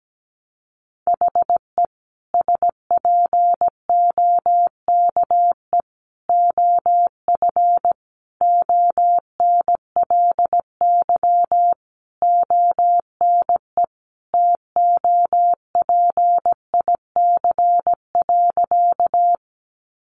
Morse code, heard as a keyed tone, about 17 wpm, 700 Hz. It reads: HE SPOKE OF ONLY ONE TOPIC.
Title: Sentences from Top 1200 Words 17wpm